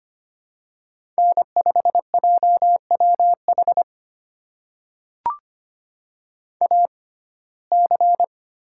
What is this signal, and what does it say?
Morse code, heard as a keyed tone, about 25 wpm, spaced as for 18 wpm, 700 Hz. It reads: N5JWH AC
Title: USA Call Signs - 5 Characters - 6x Character Spacing - 25wpm